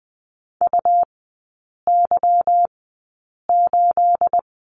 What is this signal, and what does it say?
Morse code, heard as a keyed tone, about 20 wpm, spaced as for 8 wpm, 700 Hz. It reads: UY8
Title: 212 Lesson 34e - Number 8 - Three Characters 20wpm